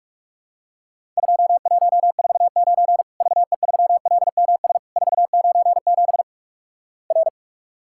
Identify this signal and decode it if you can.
Morse code, heard as a keyed tone, about 46 wpm, 700 Hz. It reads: 2149 VE3LMS 497 R
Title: Worked All Europe DX Contest - QTC Exchange - 45wpm